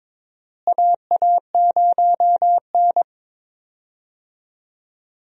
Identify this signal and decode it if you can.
Morse code, heard as a keyed tone, about 22 wpm, 700 Hz. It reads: AA0N
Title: USA Call Signs - 4 Characters - Repeated 2x - 22wpm